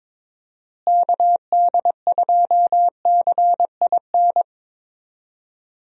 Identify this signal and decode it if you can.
Morse code, heard as a keyed tone, about 22 wpm, 700 Hz. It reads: KD2CIN
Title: USA Call Signs - 6 Characters - Repeated 3x - 22wpm